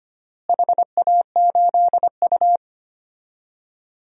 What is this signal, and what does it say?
Morse code, heard as a keyed tone, about 25 wpm, 700 Hz. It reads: HA8U
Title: DXCC-100 Call Signs - 4 Characters - Repeated 3x - 25wpm